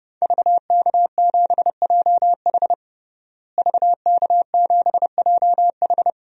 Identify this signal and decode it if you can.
Morse code, heard as a keyed tone, about 32 wpm, 700 Hz. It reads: VK7JH VK7JH
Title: DXCC-100 Call Signs - 5 Characters - Repeated 2x - 30wpm